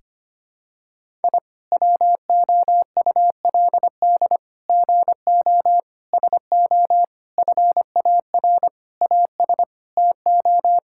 Morse code, heard as a keyed tone, about 25 wpm, 700 Hz. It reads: I WOULD GO SO FAR AS TO
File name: Sets of 9 Words - 25wpm